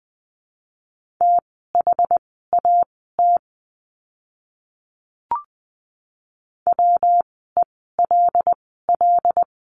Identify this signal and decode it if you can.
Morse code, heard as a keyed tone, about 20 wpm, spaced as for 14 wpm, 700 Hz. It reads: THAT WELL
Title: 115 Lesson 19f - Letter w - Words 20wpm